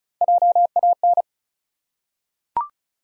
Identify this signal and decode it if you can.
Morse code, heard as a keyed tone, about 34 wpm, 700 Hz. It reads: JAN
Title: USA Common Names - Max 4 Letters - 35wpm